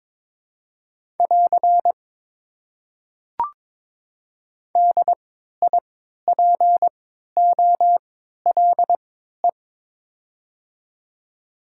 Morse code, heard as a keyed tone, about 22 wpm, spaced as for 13 wpm, 700 Hz. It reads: <AR> DIPOLE
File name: Top 90 QSO Elements - 3x Character Spacing - 22wpm